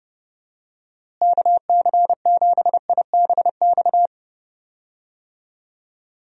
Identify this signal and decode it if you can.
Morse code, heard as a keyed tone, about 32 wpm, 700 Hz. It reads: KC7IBX
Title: USA Call Signs - 6 Characters - Repeated 3x - 30wpm